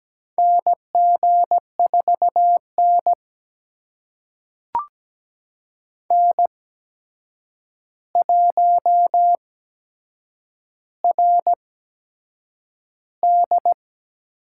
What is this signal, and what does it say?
Morse code, heard as a keyed tone, about 17 wpm, spaced as for 7 wpm, 700 Hz. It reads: NG4N N1RD
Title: USA Call Signs - 4 Characters - 8x Character Spacing - 17wpm